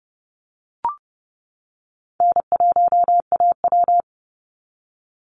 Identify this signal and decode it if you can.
Morse code, heard as a keyed tone, about 32 wpm, 700 Hz. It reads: N1AW
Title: NA SOTA Chaser Callsigns 30wpm